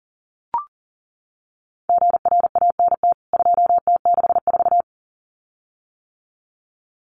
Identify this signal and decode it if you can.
Morse code, heard as a keyed tone, about 40 wpm, 700 Hz. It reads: GRANT 2T64
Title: CWT Exchange 40wpm